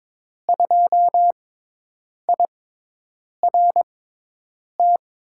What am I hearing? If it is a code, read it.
Morse code, heard as a keyed tone, about 23 wpm, spaced as for 8 wpm, 700 Hz. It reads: 2IRT
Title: USA Call Signs - 6 Characters - 6x Character Spacing - 22wpm